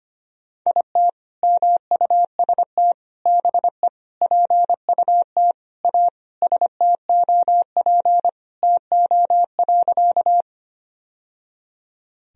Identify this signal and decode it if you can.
Morse code, heard as a keyed tone, about 25 wpm, 700 Hz. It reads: IT MUST BE PUT A STOP TO.
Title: Sentences from Top 1000 Words - Review Part 1 - 25wpm